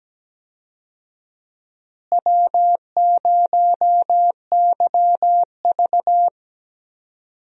Morse code, heard as a keyed tone, about 17 wpm, 700 Hz. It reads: W0YV
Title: USA Call Signs - 4 Characters - Repeated 2x - 17wpm